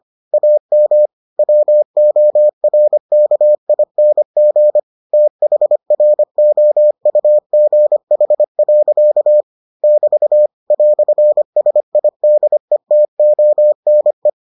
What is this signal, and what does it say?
Morse code, heard as a keyed tone, about 25 wpm, 600 Hz. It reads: AM WORKING THROUGH. <BT> "SIDETONE